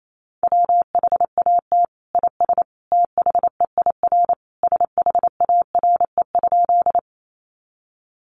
Morse code, heard as a keyed tone, about 28 wpm, 700 Hz. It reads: WHAT IS THEIR SHARE?
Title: Sentences from Top 900 Words 28wpm